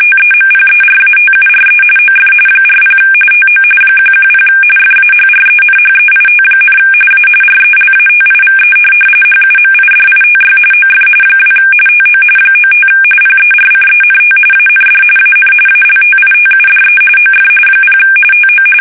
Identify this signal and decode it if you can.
RTTY, about 45.45 baud. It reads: TEMPERATURE OF 40 TO 110 FAHRENHEIT, A RELATIVE HUMIDITY OF 2 TO 95 PERCENT AND AN ALTITUDE OF 0 TO
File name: fsk850 / teletype-corp